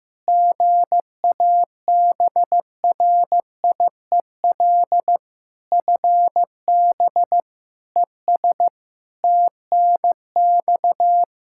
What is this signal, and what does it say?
Morse code, heard as a keyed tone, about 16 wpm, 700 Hz. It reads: GABRIEL FB ES TNX
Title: Practice QSO - Exchange 2 - Info - 15wpm